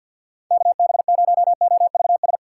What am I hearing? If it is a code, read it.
Morse code, heard as a keyed tone, about 65 wpm, 690 Hz. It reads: KB0OVS